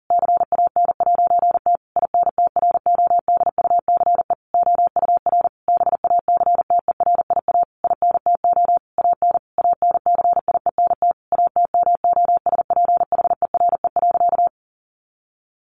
Morse code, heard as a keyed tone, about 40 wpm, 700 Hz. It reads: CAN'T INTRODUCE OUR BACTERIA INTO AN ANCIENT ATMOSPHERE.
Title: A Sound Of Thunder 40wpm